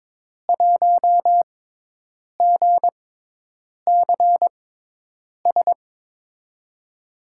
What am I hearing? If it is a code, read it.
Morse code, heard as a keyed tone, about 23 wpm, spaced as for 8 wpm, 700 Hz. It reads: 1GCS
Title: USA Call Signs - 6x Character Spacing - 22wpm